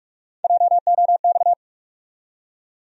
Morse code, heard as a keyed tone, about 48 wpm, 700 Hz. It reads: JOX
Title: USA Call Sign Postfixes - 3 Letters 45wpm